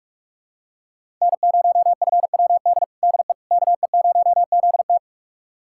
Morse code, heard as a keyed tone, about 43 wpm, 700 Hz. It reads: N0FWD DE KE0ZT
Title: USA Callsign DE Callsign - 5 Characters - 45wpm